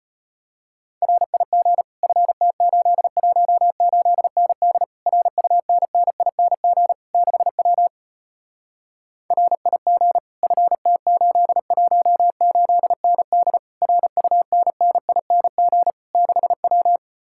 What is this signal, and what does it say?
Morse code, heard as a keyed tone, about 33 wpm, 700 Hz. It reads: RIG FT818ND RUNNING 6W RIG FT818ND RUNNING 6W